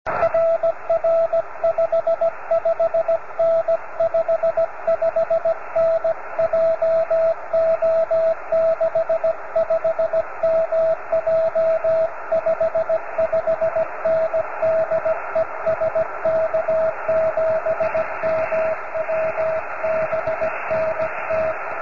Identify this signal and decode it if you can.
Morse code, heard as a keyed tone, about 17 wpm, 650 Hz, with radio noise. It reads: RR 55N 55N JO65MJ 55N DE SK7MW BNT